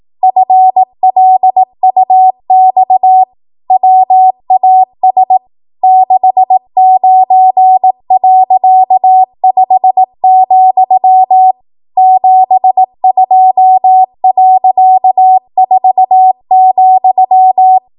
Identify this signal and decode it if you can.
Morse code, heard as a keyed tone, about 18 wpm, 750 Hz. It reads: FLUX WAS 69.5, 72.4,
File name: Archive / 18 WPM / 18WPM